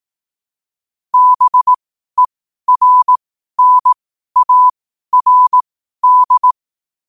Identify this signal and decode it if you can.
Morse code, heard as a keyed tone, about 18 wpm, spaced as for 12 wpm, 1000 Hz. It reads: BERNARD
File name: qotdfiles / 12wpm